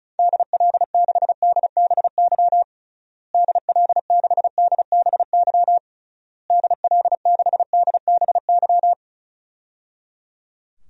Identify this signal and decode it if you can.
Morse code, heard as a keyed tone, about 34 wpm, 700 Hz. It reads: DL6DBY DL6DBY DL6DBY
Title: DXCC-100 Call Signs - 6 Characters - Repeated 3x - 35wpm